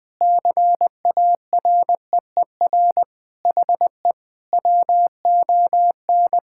Morse code, heard as a keyed tone, about 21 wpm, 700 Hz. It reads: CAREER HE WON
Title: Sentences with 7-Word Phrases - 20wpm